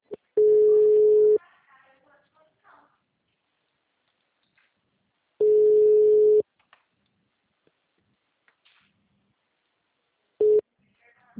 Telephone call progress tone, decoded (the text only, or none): ringback tone